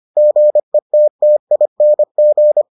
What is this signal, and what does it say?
Morse code, heard as a keyed tone, about 25 wpm, 600 Hz. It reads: GETTING